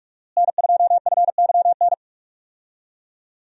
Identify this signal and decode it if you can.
Morse code, heard as a keyed tone, about 47 wpm, 700 Hz. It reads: N2FYN
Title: USA Call Signs - 5 Characters - Repeated 2x - 45wpm